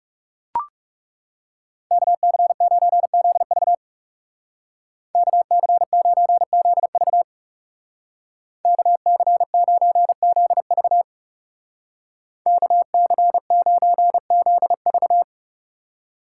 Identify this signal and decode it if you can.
Morse code, heard as a keyed tone, about 34 wpm, 700 Hz. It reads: KC9ZV KC9ZV KC9ZV KC9ZV
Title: Speed Racing - USA 2x2 Call Signs 30wpm